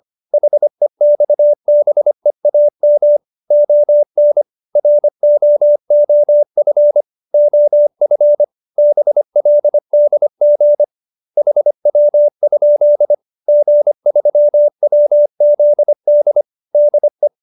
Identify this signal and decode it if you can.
Morse code, heard as a keyed tone, about 25 wpm, 590 Hz. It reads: HEXBEAM ON ROOF OF BLDG HW? G3WZD DE
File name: QSO310 25